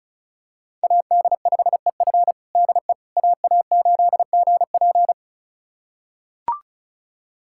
Morse code, heard as a keyed tone, about 36 wpm, 700 Hz. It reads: AD5EF DE AA8GP
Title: USA Callsign DE Callsign - 5 Characters - 35wpm